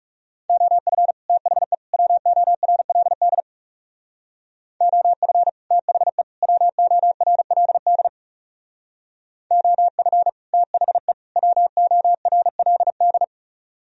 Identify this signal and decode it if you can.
Morse code, heard as a keyed tone, about 37 wpm, 700 Hz. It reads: OF THE WORLD OF THE WORLD OF THE WORLD
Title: Speed-Racing - Sets of 3 Words 30wpm